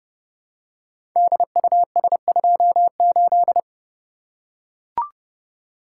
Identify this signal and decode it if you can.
Morse code, heard as a keyed tone, about 30 wpm, 700 Hz. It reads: DUS28
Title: Code Groups - 5 Letter:Numbers - 30wpm